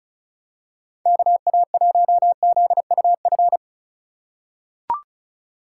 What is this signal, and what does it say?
Morse code, heard as a keyed tone, about 36 wpm, 700 Hz. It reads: KA1ZUF
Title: USA Call Signs - 6 Characters - 4x Character Spacing - 35wpm